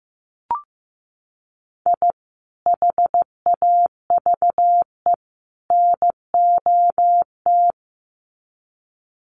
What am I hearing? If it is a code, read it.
Morse code, heard as a keyed tone, about 15 wpm, 700 Hz. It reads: I HAVE NOT
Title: Sets of 3 Words 15wpm